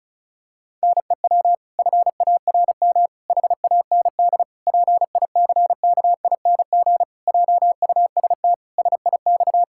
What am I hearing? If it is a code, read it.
Morse code, heard as a keyed tone, about 36 wpm, 700 Hz. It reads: NEW FARM HAND PICKING JUST SIX
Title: Pangram Challenge v2 - 35wpm